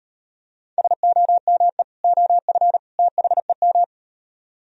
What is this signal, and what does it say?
Morse code, heard as a keyed tone, about 38 wpm, 700 Hz. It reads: SOME OF THEM